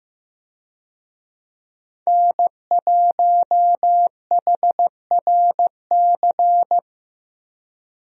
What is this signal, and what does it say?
Morse code, heard as a keyed tone, about 15 wpm, 700 Hz. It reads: N1HRC
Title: USA Call Signs - 5 Characters - Repeated 3x - 15wpm